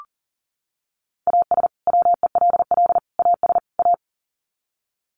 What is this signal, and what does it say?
Morse code, heard as a keyed tone, about 40 wpm, 700 Hz. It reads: AS WELL AS A
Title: Sets of 4 Words 40wpm